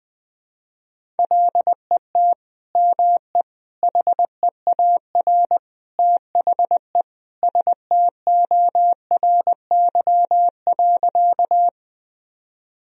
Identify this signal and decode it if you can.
Morse code, heard as a keyed tone, about 21 wpm, 700 Hz. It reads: LET ME HEAR THE STORY.
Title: Sentences from Top 1000 Words - Review Part 1 - 20wpm